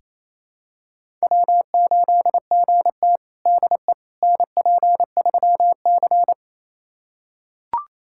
Morse code, heard as a keyed tone, about 28 wpm, 700 Hz. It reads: W8GT DE NP3C
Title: USA Callsign DE Callsign - 4 Characters - 28wpm